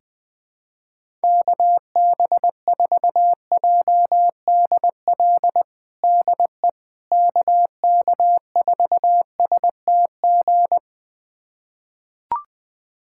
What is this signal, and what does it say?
Morse code, heard as a keyed tone, about 20 wpm, 700 Hz. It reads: KB4JDL DE KK4STG
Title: USA Callsign DE Callsign 20wpm